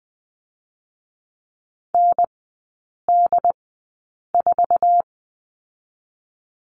Morse code, heard as a keyed tone, about 20 wpm, 700 Hz. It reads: N D 4
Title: 127 Lesson 21e - Number 6 - Three Characters 20wpm